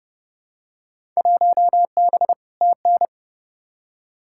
Morse code, heard as a keyed tone, about 32 wpm, 700 Hz. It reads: 1B TN
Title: Field Day Exchanges - 30wpm